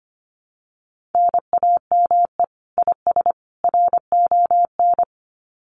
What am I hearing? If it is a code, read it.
Morse code, heard as a keyed tone, about 25 wpm, 700 Hz. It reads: NAME IS RON
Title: Common QSO Phrases 25wpm